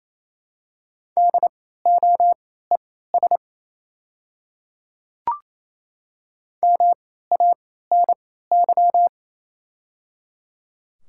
Morse code, heard as a keyed tone, about 28 wpm, spaced as for 16 wpm, 700 Hz. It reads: DOES MANY
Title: 3x Character Spacing - Top 200 Words - 28wpm